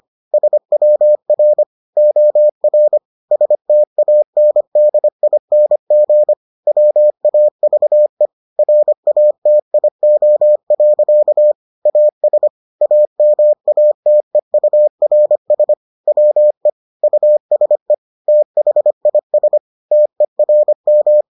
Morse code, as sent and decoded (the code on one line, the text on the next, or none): ... .-- .-. / --- .-. / ... - .- -. -.. .. -. --. / .-- .- ...- . / .-. .- - .. --- .-.-.- / .- ... / .- -- .- - . ..- .-. ... / .-- . / ..- ... / - .... .. ... / - . .-. --
SWR OR STANDING WAVE RATIO. AS AMATEURS WE USE THIS TERM